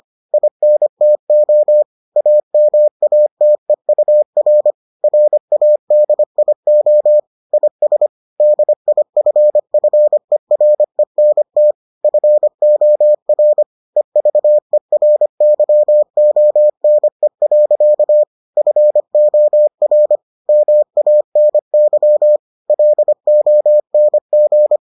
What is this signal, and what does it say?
Morse code, heard as a keyed tone, about 25 wpm, 600 Hz. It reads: INTO AMATEUR RADIO IS DIFFERENT FOR EVERYONE. FOR MANY LONG